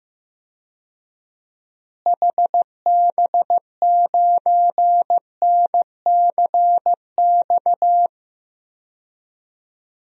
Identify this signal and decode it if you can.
Morse code, heard as a keyed tone, about 16 wpm, 700 Hz. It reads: HB9NCX
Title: DXCC-100 Call Signs - 6 Characters - Repeated 2x - 15wpm